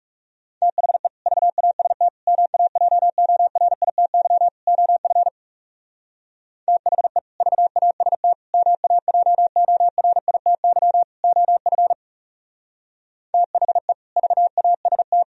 Morse code, heard as a keyed tone, about 36 wpm, 700 Hz. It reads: THE VAST MAJORITY OF THE VAST MAJORITY OF THE VAST
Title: Speed-Racing - Sets of 4 Words 30wpm